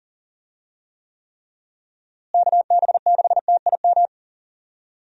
Morse code, heard as a keyed tone, about 40 wpm, 700 Hz. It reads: KB6TIM
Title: USA Call Signs - 6 Characters - Repeated 3x - 40wpm